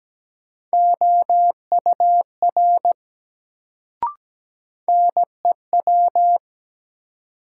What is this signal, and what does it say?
Morse code, heard as a keyed tone, about 17 wpm, 700 Hz. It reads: OUR NEW
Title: Sets of 2 Words - New Word Drill - 17wpm